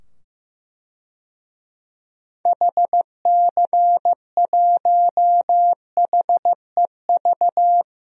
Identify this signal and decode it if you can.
Morse code, heard as a keyed tone, about 15 wpm, 700 Hz. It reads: HC1HEV
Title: DXCC-100 Call Signs - 6 Characters - 15wpm